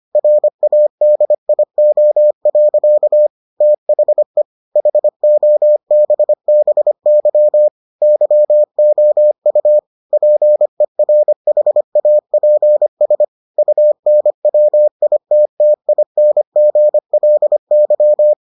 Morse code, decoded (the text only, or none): RADIO. THE HOBBY YOU PERHAPS UNWITTINGLY